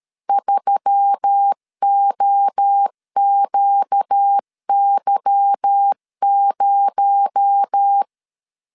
Morse code, heard as a keyed tone, about 13 wpm, 800 Hz. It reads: 3OQY0
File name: CW / DET / 1 9 1500C S C 12wpm 3811